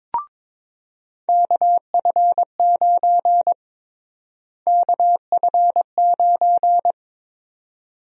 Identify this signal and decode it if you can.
Morse code, heard as a keyed tone, about 22 wpm, 700 Hz. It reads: KF9 KF9
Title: USA Call Sign Prefixes - Repeated 2x - 22wpm